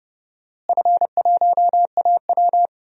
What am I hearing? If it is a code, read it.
Morse code, heard as a keyed tone, about 30 wpm, 700 Hz. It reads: F1AW
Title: DXCC-100 Call Signs - 4 Characters - Repeated 2x - 30wpm